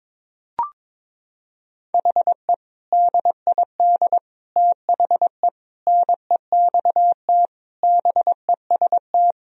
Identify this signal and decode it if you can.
Morse code, heard as a keyed tone, about 21 wpm, 700 Hz. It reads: HE DID THE NEXT BEST
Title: Sentences from Top 1000 Words - Review Part 1 - 22wpm